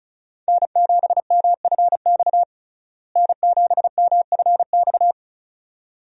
Morse code, heard as a keyed tone, about 36 wpm, 700 Hz. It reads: N7MFX N7MFX
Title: USA Call Signs - 5 Characters - Repeated 2x - 35wpm